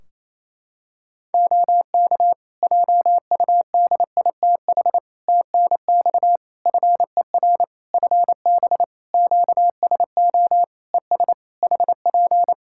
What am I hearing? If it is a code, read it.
Morse code, heard as a keyed tone, about 28 wpm, 700 Hz. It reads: OK JUDITH TNX FER FB QSO ES HP
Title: Practice QSO - Exchange 4 - Ending - 28wpm